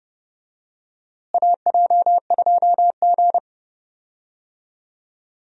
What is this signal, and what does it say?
Morse code, heard as a keyed tone, about 30 wpm, 700 Hz. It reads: AJ2G
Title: USA Call Signs - 4 Characters - Repeated 2x - 30wpm